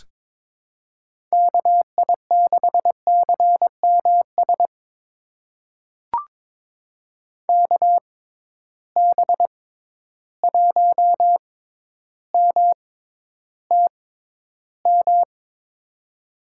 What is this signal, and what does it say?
Morse code, heard as a keyed tone, about 21 wpm, spaced as for 11 wpm, 700 Hz. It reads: KI6CMS KB1MTM
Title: USA Call Signs - 6 Characters - 6x Character Spacing - 22wpm